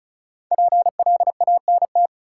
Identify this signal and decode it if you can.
Morse code, heard as a keyed tone, about 36 wpm, 700 Hz. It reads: PLANT
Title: Sentences from Top 1000 Words - Review Part 1 - New Word Drill - 35wpm